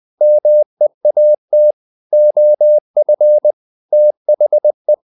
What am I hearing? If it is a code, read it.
Morse code, heard as a keyed tone, about 20 wpm, 600 Hz. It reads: MEAT OF THE
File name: the old man and the sea part 2 20wpm